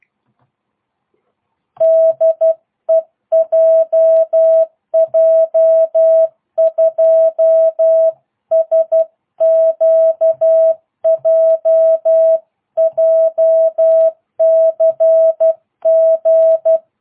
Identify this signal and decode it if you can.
Morse code, heard as a keyed tone, about 12 wpm, 650 Hz. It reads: DE JJ2SQJ JCG